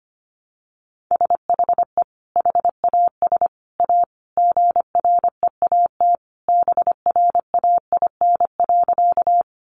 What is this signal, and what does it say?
Morse code, heard as a keyed tone, about 25 wpm, 700 Hz. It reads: SHE HAS A GREAT BRAIN.
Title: Sentences from Top 1100 Words 25wpm